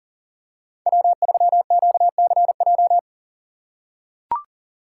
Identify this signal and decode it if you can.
Morse code, heard as a keyed tone, about 40 wpm, 700 Hz. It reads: W3QCJ
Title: USA Call Signs - 5 Characters - 3x Character Spacing - 40wpm